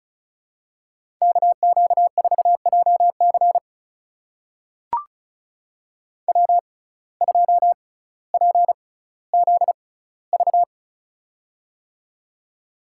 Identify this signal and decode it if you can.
Morse code, heard as a keyed tone, about 36 wpm, spaced as for 18 wpm, 700 Hz. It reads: KQ4JC W2PZV
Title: USA Call Signs - 5 Characters - 6x Character Spacing - 35wpm